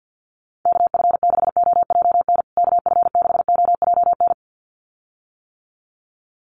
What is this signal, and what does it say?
Morse code, heard as a keyed tone, about 50 wpm, 700 Hz. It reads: KF6OJN KF6OJN
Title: USA Call Signs Repeated 2x 50wpm